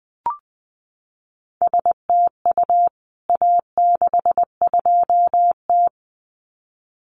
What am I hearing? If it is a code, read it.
Morse code, heard as a keyed tone, about 20 wpm, 700 Hz. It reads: STU A62T
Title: CWT Exchange 20wpm